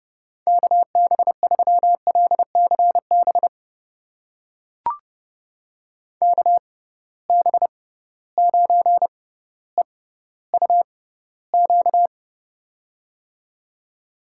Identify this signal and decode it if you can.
Morse code, heard as a keyed tone, about 30 wpm, spaced as for 15 wpm, 700 Hz. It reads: KB3LCB KB9EUQ
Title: USA Call Signs - 6 Characters - 6x Character Spacing - 30wpm